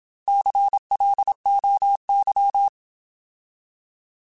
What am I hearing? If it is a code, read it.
Morse code, heard as a keyed tone, about 27 wpm, 800 Hz. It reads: CLOY
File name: Alphabetical